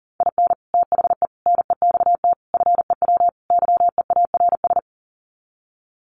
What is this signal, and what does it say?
Morse code, heard as a keyed tone, about 37 wpm, 700 Hz. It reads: IN THE NEXT FEW YEARS
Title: Sets of 5 Words Encore 40wpm